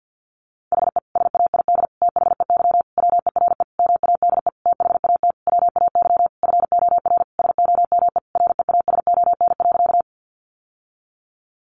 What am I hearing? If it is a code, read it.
Morse code, heard as a keyed tone, about 50 wpm, 700 Hz. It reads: HE SAID THEY WERE MADE THAT WAY FOR SOME REASON.
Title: Sentences from Top 700 Words 50wpm